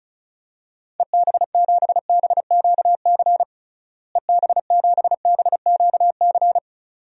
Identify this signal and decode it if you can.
Morse code, heard as a keyed tone, about 36 wpm, 700 Hz. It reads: EB7BQC EB7BQC
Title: DXCC-100 Call Signs - 6 Characters - Repeated 2x - 35wpm